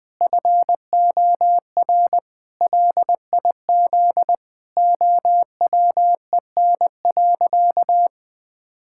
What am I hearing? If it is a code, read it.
Morse code, heard as a keyed tone, about 20 wpm, 700 Hz. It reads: FOR LIZ OWEN.
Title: Pangram Challenge v2 - 20wpm